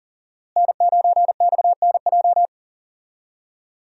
Morse code, heard as a keyed tone, about 40 wpm, 700 Hz. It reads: N9XNJ
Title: USA Call Signs - 5 Characters - Repeated 3x - 40wpm